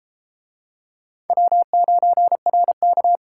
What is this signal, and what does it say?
Morse code, heard as a keyed tone, about 33 wpm, 700 Hz. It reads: W9RK